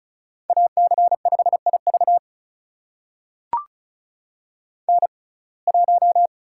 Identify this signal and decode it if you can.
Morse code, heard as a keyed tone, about 36 wpm, spaced as for 25 wpm, 700 Hz. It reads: AC5IV N1
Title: USA Call Signs - 5 Characters - 6x Character Spacing - 35wpm